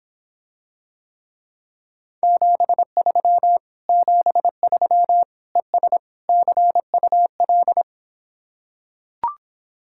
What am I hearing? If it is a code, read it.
Morse code, heard as a keyed tone, about 26 wpm, 700 Hz. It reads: 73 73 ES CUL